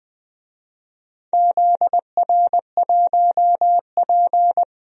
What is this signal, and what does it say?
Morse code, heard as a keyed tone, about 21 wpm, 700 Hz. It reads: ZR1P
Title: DXCC-100 Call Signs - 4 Characters - 20wpm